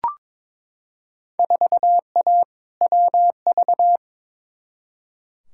Morse code, heard as a keyed tone, about 22 wpm, 700 Hz. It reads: 4A WV
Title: Field Day Exchanges Encore - 22wpm